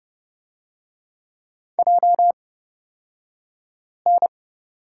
Morse code, heard as a keyed tone, about 29 wpm, spaced as for 5 wpm, 700 Hz. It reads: JN
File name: Lesson 36d - BT - Two Characters 30wpm